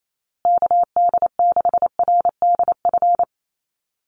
Morse code, heard as a keyed tone, about 28 wpm, 700 Hz. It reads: KD6RDF
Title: USA Call Signs Repeated 3x 28wpm